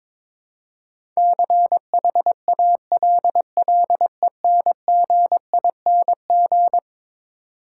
Morse code, heard as a keyed tone, about 22 wpm, 700 Hz. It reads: CHALLENGING
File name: practice / 22wpm / Eleven Letter Words - 22wpm